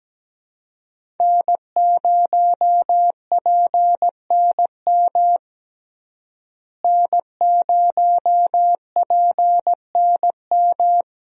Morse code, heard as a keyed tone, about 17 wpm, 700 Hz. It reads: N0PNM N0PNM
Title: USA Call Signs - 5 Characters - Repeated 2x - 17wpm